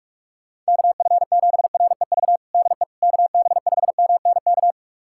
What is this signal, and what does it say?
Morse code, heard as a keyed tone, about 46 wpm, 700 Hz. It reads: KF7REV DE KB5MNK
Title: USA Callsign DE Callsign - 6 Characters - 45wpm